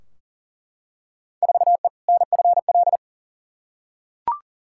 Morse code, heard as a keyed tone, about 40 wpm, 700 Hz. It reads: 4E NFL
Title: Field Day Exchanges Encore - 40wpm